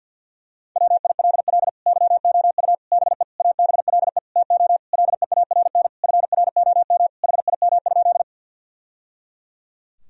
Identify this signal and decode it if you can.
Morse code, heard as a keyed tone, about 52 wpm, 700 Hz. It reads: WILL YOU BE ABLE TO LEARN FROM HIM?
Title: Sentences from Top 1000 Words - Review Part 2 - 50wpm